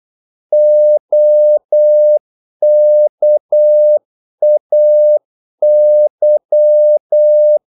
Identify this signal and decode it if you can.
Morse code, heard as a keyed tone, about 8 wpm, 600 Hz. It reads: OKAY
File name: Korte ord DK 8wpm